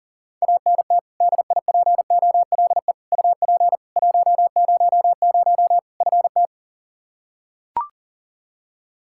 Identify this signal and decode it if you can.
Morse code, heard as a keyed tone, about 40 wpm, 700 Hz. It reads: ANT DIPOLE UP 100 FT
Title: Practice QSO - Exchange 2 - Info - 40wpm